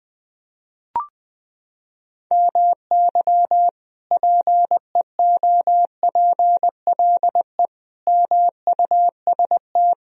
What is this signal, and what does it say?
Morse code, heard as a keyed tone, about 20 wpm, 700 Hz. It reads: MY PEOPLE MUST
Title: Sentences from Top 1000 Words - Review Part 3 - 20wpm